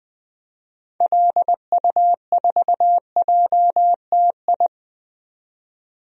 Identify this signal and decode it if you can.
Morse code, heard as a keyed tone, about 20 wpm, 700 Hz. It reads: LU4JTI
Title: DXCC-100 Call Signs - 6 Characters - 20wpm